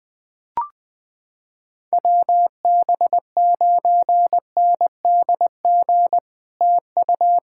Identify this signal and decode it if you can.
Morse code, heard as a keyed tone, about 21 wpm, 700 Hz. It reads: WB9NDG TU
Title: POTA QSO Practice - 20wpm